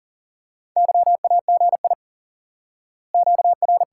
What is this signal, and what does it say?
Morse code, heard as a keyed tone, about 40 wpm, 700 Hz. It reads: YAGI QR
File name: IQE Course 3 - Lesson 89 - YAGI - Two Elements - 40wpm